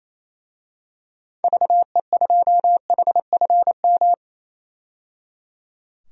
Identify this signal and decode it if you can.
Morse code, heard as a keyed tone, about 28 wpm, 700 Hz. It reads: VE2HFM
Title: DXCC-100 Call Signs - 6 Characters - 28wpm